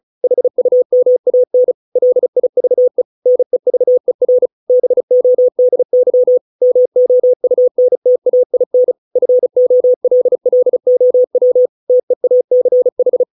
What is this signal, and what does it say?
Morse code, heard as a keyed tone, about 34 wpm, 500 Hz. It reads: HUMAN LIVE NEVER BODY MOUNTAIN FOLLOW TEACH